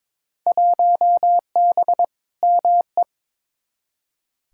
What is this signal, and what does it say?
Morse code, heard as a keyed tone, about 22 wpm, 700 Hz. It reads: 1B ME
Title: Field Day Exchanges - 22wpm